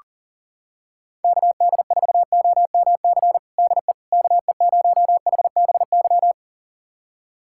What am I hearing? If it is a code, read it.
Morse code, heard as a keyed tone, about 37 wpm, 700 Hz. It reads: KD4OMC DE KE0HBY
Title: USA Callsign DE Callsign - 6 Characters - 40wpm